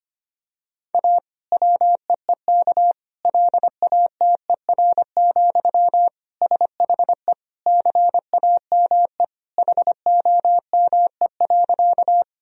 Morse code, heard as a keyed tone, about 25 wpm, 700 Hz. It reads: A WEEK LATER, SHE CAME HOME.
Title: Sentences from Top 1000 Words - Review Part 4 - 25wpm